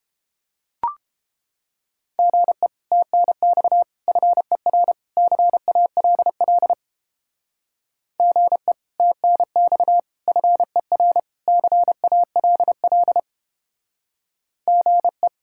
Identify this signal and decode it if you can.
Morse code, heard as a keyed tone, about 29 wpm, 700 Hz. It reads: GE TNX FER CALL GE TNX FER CALL GE